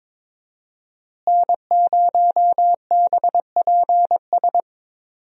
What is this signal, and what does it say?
Morse code, heard as a keyed tone, about 22 wpm, 700 Hz. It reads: N0BPS
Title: USA Call Signs - 5 Characters - Repeated 2x - 22wpm